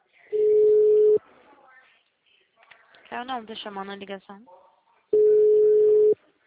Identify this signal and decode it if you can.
Ringback tone (Europe, 425 Hz), with speech.